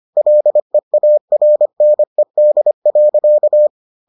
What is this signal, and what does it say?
Morse code, heard as a keyed tone, about 25 wpm, 600 Hz. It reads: LEARNED.